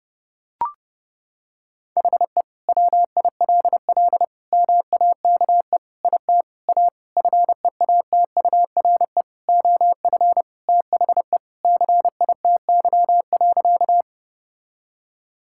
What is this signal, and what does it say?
Morse code, heard as a keyed tone, about 30 wpm, 700 Hz. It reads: HE WILL MAKE IT A FEATURE OF THE CITY.
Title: Sentences from Top 1000 Words - Review Part 5 - 30wpm